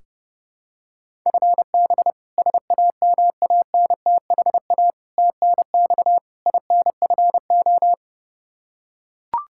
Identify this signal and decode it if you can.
Morse code, heard as a keyed tone, about 32 wpm, 700 Hz. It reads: FB SAMANTHA TNX INFO
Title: K1USN SST QSO Practice 30wpm